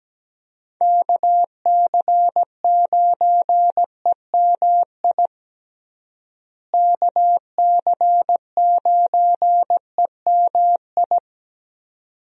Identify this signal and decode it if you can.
Morse code, heard as a keyed tone, about 17 wpm, 700 Hz. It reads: KC9EMI KC9EMI
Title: USA Call Signs - 6 Characters - Repeated 2x - 17wpm